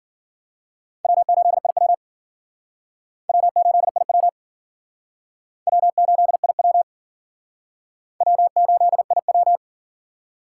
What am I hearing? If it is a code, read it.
Morse code, heard as a keyed tone, about 53 wpm, 700 Hz. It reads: W8IW W8IW W8IW W8IW